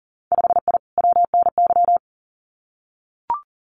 Morse code, heard as a keyed tone, about 40 wpm, 700 Hz. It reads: 5I WNY
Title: Winter Field Day Exchanges Encore 40wpm